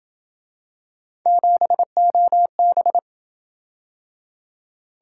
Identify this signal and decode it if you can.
Morse code, heard as a keyed tone, about 27 wpm, 700 Hz. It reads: 7OB